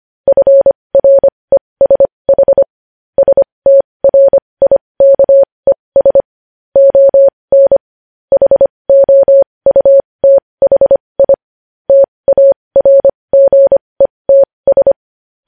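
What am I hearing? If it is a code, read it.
Morse code, heard as a keyed tone, about 25 wpm, spaced as for 20 wpm, 550 Hz. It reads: FRESH STRIKES ON HOUTHI TARGETS
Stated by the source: Morse code news headlines practice at 20 wpm.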